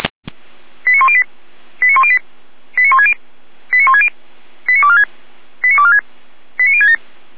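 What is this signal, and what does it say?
Five-tone selective call (ZVEI), heard as quick runs of five tones, tones of about 70 ms. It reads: 89108 89108 89160 89160 89246 89246 89067
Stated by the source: ZVEI five-tone selective call 89223, received by SDR.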